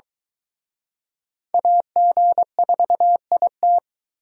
Morse code, heard as a keyed tone, about 22 wpm, 700 Hz. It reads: AG4IT